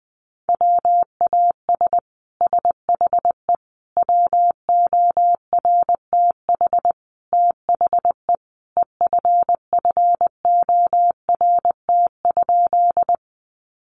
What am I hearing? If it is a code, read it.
Morse code, heard as a keyed tone, about 20 wpm, 700 Hz. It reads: WAS SHE WORTH THE EFFORT?
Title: Sentences from Top 1200 Words 20wpm